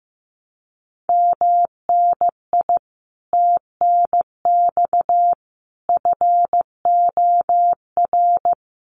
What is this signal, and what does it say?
Morse code, heard as a keyed tone, about 15 wpm, 700 Hz. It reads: MNI TNX FOR
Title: Common QSO Phrases 15wpm